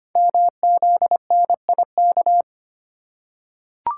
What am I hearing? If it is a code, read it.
Morse code, heard as a keyed tone, about 25 wpm, 700 Hz. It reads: MZNIK
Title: Code Groups - 5 Letters - 25wpm